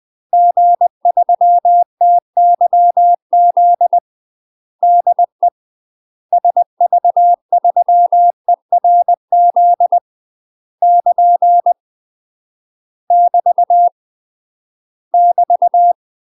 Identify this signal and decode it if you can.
Morse code, heard as a keyed tone, about 20 wpm, 700 Hz. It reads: G3TYZ DE SV3ERZ <KN> <BT> <BT>
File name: UK MASTER QSOs TRAINING / 20WPM / UK MASTER QSOs 0008